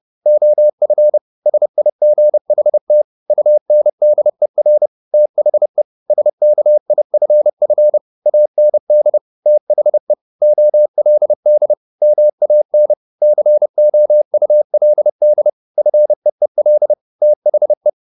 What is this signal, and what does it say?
Morse code, heard as a keyed tone, about 30 wpm, 600 Hz. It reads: OF SIGHT UNDER THE SKIFF AND THE OLD MAN COULD FEEL THE